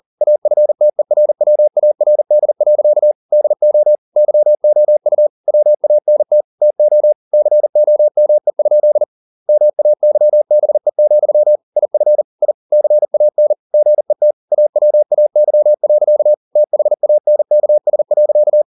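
Morse code, heard as a keyed tone, about 40 wpm, 600 Hz. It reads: AFTERWARD. DO YOU WANT TO COME? MAYBE, IF I CAN GET AWAY. THANKS.
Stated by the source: Morse code at 40 wpm, practice text, CW med Gnister.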